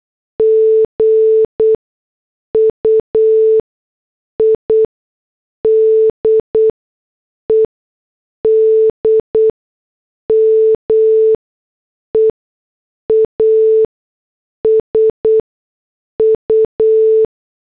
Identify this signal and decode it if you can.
Morse code, heard as a keyed tone, about 8 wpm, spaced as for 6 wpm, 440 Hz. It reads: GUIDED MEASU
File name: morse / headlines